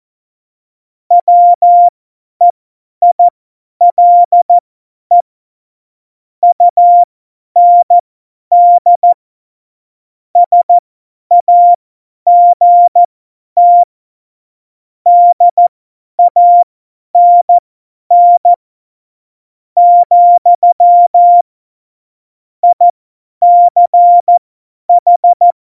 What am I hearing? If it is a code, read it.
Morse code, heard as a keyed tone, about 14 wpm, spaced as for 10 wpm, 700 Hz. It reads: WEILE UND SAGT DANN , ICH
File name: Morsekurs / Witze / Witz 31.Juli 2018, 50 F70